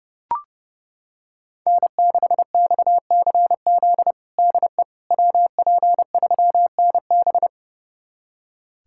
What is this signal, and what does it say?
Morse code, heard as a keyed tone, about 30 wpm, 700 Hz. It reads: N6XCZ DE WP3NB
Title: USA Callsign DE Callsign - 5 Characters - 30wpm